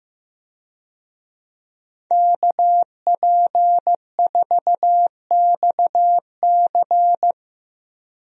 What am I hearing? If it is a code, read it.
Morse code, heard as a keyed tone, about 16 wpm, 700 Hz. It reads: KP4XC
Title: DXCC-100 Call Signs - 5 Characters - Repeated 3x - 15wpm